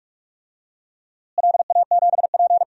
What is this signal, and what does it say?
Morse code, heard as a keyed tone, about 46 wpm, 700 Hz. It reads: LA7P
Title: DXCC-100 Call Signs - 4 Characters - Repeated 2x - 45wpm